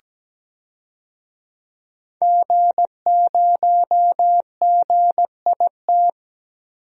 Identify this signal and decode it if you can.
Morse code, heard as a keyed tone, about 17 wpm, 700 Hz. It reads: G0GIT
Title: DXCC-100 Call Signs - 5 Characters - Repeated 2x - 17wpm